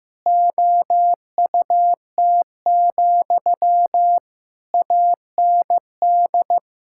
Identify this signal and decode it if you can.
Morse code, heard as a keyed tone, about 15 wpm, 700 Hz. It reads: OUT, AND
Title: Sentences from Top 1000 Words - Review Part 6 - 15wpm